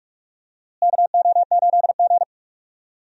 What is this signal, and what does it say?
Morse code, heard as a keyed tone, about 46 wpm, 700 Hz. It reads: KO8G
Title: USA Call Signs - 4 Characters - Repeated 2x - 45wpm